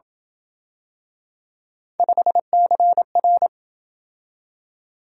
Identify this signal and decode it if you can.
Morse code, heard as a keyed tone, about 26 wpm, 700 Hz. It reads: 5CR